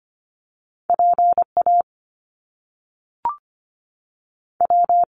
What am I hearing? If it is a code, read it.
Morse code, heard as a keyed tone, about 25 wpm, 700 Hz. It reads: PA W
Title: US States 25wpm